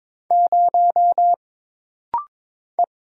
Morse code, heard as a keyed tone, about 22 wpm, 700 Hz. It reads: 0 E
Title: Single Letter-Number - Rapid-Fire 22wpm